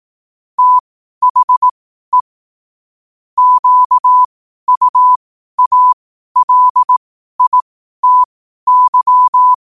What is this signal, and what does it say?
Morse code, heard as a keyed tone, about 18 wpm, spaced as for 12 wpm, 1000 Hz. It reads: THE QUALITY
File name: qotdfiles / 12wpm